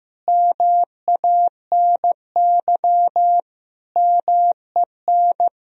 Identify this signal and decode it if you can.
Morse code, heard as a keyed tone, about 16 wpm, 700 Hz. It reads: MANY MEN
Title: Sentences from Top 1000 Words - Review Part 1 - 15wpm